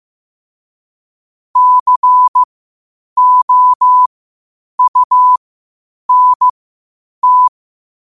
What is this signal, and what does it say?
Morse code, heard as a keyed tone, about 15 wpm, spaced as for 8 wpm, 1000 Hz. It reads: COUNT
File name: qotdfiles / 8wpm